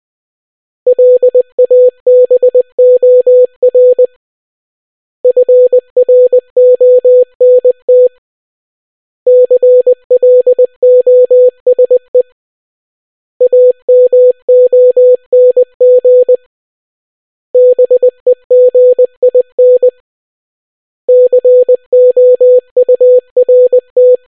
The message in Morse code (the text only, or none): LABOR FRONT CLOSE AMONG BEGIN COURT